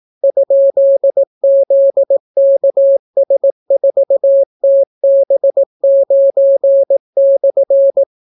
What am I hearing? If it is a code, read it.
Morse code, heard as a keyed tone, about 18 wpm, 550 Hz. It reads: ?ZKS4TB9/